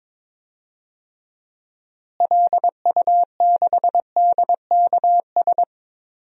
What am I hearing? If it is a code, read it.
Morse code, heard as a keyed tone, about 22 wpm, 700 Hz. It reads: LU6DKS
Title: DXCC-100 Call Signs - 6 Characters - Repeated 3x - 22wpm